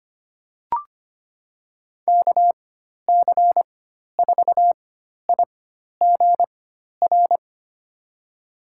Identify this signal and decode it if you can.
Morse code, heard as a keyed tone, about 25 wpm, spaced as for 12 wpm, 700 Hz. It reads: KC4IGR
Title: USA Call Signs - 6 Characters - 4x Character Spacing - 25wpm